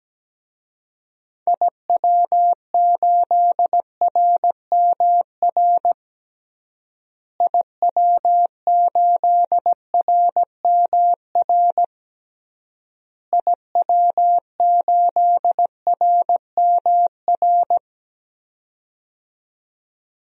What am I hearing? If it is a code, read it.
Morse code, heard as a keyed tone, about 17 wpm, 700 Hz. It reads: IW8RMR IW8RMR IW8RMR
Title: DXCC-100 Call Signs - 6 Characters - Repeated 3x - 17wpm